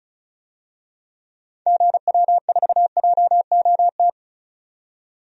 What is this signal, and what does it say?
Morse code, heard as a keyed tone, about 36 wpm, 700 Hz. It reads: GW4JOT
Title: DXCC-100 Call Signs - 6 Characters - 35wpm